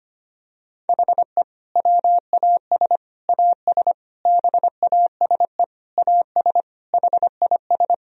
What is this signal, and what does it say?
Morse code, heard as a keyed tone, about 25 wpm, 700 Hz. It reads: HE WAS AS BASE AS HIS
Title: Sentences from Top 1000 Words - Review Part 1 - 25wpm